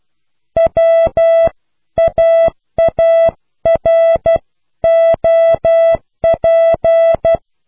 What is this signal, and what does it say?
Morse code, heard as a keyed tone, about 12 wpm, 660 Hz. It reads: WAAROP